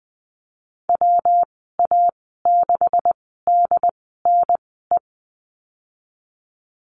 Morse code, heard as a keyed tone, about 20 wpm, spaced as for 15 wpm, 700 Hz. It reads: WA6DNE
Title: 147 Lesson 24g - Letter y - Call Signs 20wpm